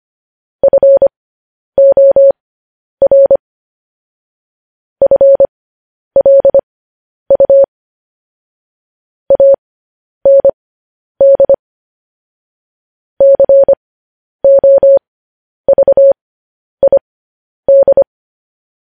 Morse code, heard as a keyed tone, about 25 wpm, spaced as for 10 wpm, 550 Hz. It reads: FOR FLU AND COVID
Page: news headlines in morse code at 10 wpm